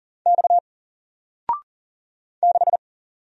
Morse code, heard as a keyed tone, about 40 wpm, 700 Hz. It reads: X 6
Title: Single Character (Letter:Number:Punctuation:Prosign) Alt 40wpm